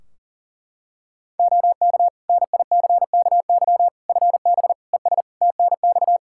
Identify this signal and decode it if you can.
Morse code, heard as a keyed tone, about 40 wpm, 700 Hz. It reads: OK NICKY FB ES TNX
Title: Practice QSO - Exchange 2 - Info - 40wpm